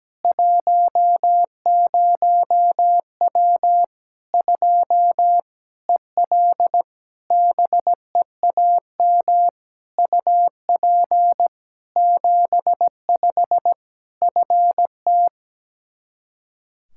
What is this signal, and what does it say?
Morse code, heard as a keyed tone, about 17 wpm, 700 Hz. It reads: 10W 2 EL BEAM UP 75 FT
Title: K1USN SST QSO Practice 17wpm